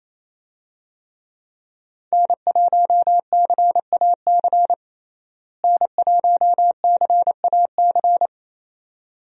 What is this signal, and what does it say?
Morse code, heard as a keyed tone, about 28 wpm, 700 Hz. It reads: N1CAC N1CAC
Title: USA Call Signs - 5 Characters - Repeated 2x - 28wpm